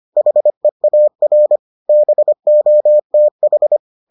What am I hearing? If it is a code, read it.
Morse code, heard as a keyed tone, about 25 wpm, 600 Hz. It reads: HEAR BOTH